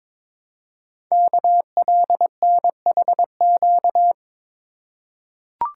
Morse code, heard as a keyed tone, about 23 wpm, 700 Hz. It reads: KLNHQ
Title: Code Groups - 5 Letters - 22wpm